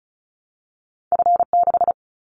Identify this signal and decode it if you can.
Morse code, heard as a keyed tone, about 34 wpm, 700 Hz. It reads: F6
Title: Callsign Prefixes for Top 25 DXCC - Repeated 3x - 35wpm